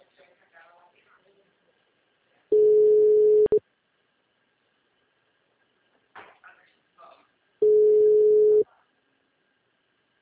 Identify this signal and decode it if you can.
Ringback tone (Europe, 425 Hz), with speech.